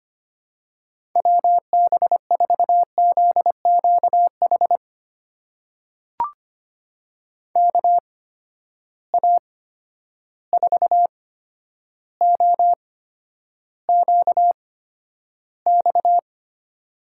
Morse code, heard as a keyed tone, about 25 wpm, spaced as for 11 wpm, 700 Hz. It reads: WB4ZQH KA4OQX
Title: USA Call Signs - 6 Characters - 8x Character Spacing - 25wpm